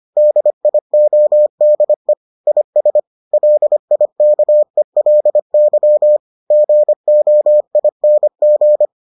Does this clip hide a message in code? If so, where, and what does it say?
Yes: Morse code from 0.2 to 8.8 s, heard as a keyed tone, about 25 wpm, 600 Hz. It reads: DIODE IS LIKELY GOING